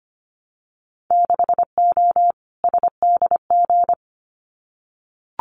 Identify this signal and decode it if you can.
Morse code, heard as a keyed tone, about 25 wpm, 700 Hz. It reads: 6O SDG
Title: Winter Field Day Exchanges Encore 25wpm